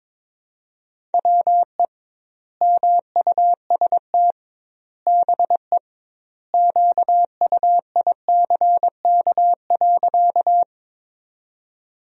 Morse code, heard as a keyed tone, about 22 wpm, 700 Hz. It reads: WE MUST BE QUICK.
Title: 2x Word-Spacing - Sentences from Top 500 Words - 22wpm